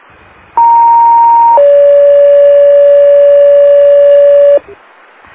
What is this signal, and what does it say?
Two-tone page, heard as a one-second tone, then a three-second tone, about 930 then 570 Hz. It is heard from 0.6 to 4.6 s.